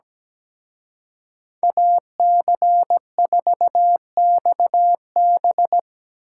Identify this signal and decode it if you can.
Morse code, heard as a keyed tone, about 20 wpm, 700 Hz. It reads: AC4XB
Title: Speed Racing - USA Call Signs 17wpm